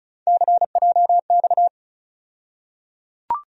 Morse code, heard as a keyed tone, about 34 wpm, 700 Hz. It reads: CJX
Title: USA Call Sign Postfixes - 3 Letters 35wpm